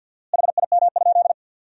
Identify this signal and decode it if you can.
Morse code, heard as a keyed tone, about 53 wpm, 700 Hz. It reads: HIM?
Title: 4x Word-Spacing - Sentences from Top 200 Words - 50wpm